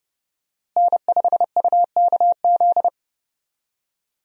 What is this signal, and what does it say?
Morse code, heard as a keyed tone, about 30 wpm, 700 Hz. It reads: N5UKZ
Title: USA Call Signs - 5 Characters - Repeated 2x - 30wpm